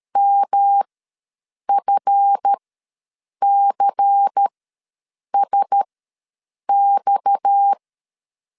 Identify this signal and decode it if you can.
Morse code, heard as a keyed tone, about 13 wpm, spaced as for 7 wpm, 800 Hz. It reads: MFCSX
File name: LEZIONI / 7 GF9Z / 7 GF ws 12wpm 3140